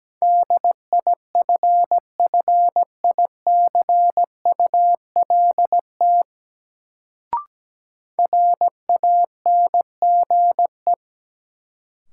Morse code, heard as a keyed tone, about 17 wpm, 700 Hz. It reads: DIFFICULT RANGE
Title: Sets of 4 Words - New Word Drill - 17wpm